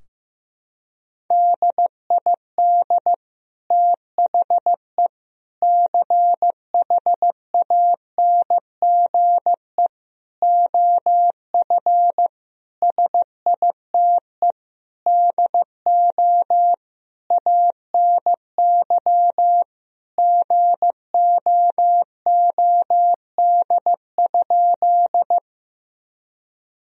Morse code, heard as a keyed tone, about 15 wpm, 700 Hz. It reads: DID THE CHANGE OF SITE DO ANY GOOD?
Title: Sentences from Top 1000 Words - Review Part 4 - 15wpm